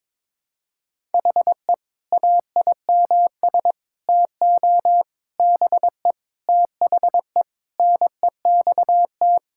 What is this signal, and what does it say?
Morse code, heard as a keyed tone, about 22 wpm, 700 Hz. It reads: HE AIMS TO BE THE NEXT
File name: practice / 22wpm / Sentences with 8-Word Phrases - 22wpm